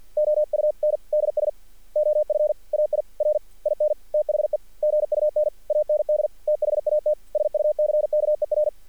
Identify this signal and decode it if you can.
Morse code, heard as a keyed tone, about 48 wpm, 590 Hz. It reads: OUNDS OF AIR IN THE GUN AND THAT SUCKER
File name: archived / CRSnet